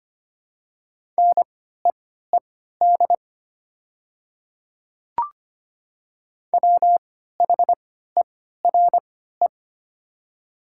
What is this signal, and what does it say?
Morse code, heard as a keyed tone, about 25 wpm, spaced as for 14 wpm, 690 Hz. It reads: NEED WHERE
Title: 3x Character Spacing - Top 200 Words - 25wpm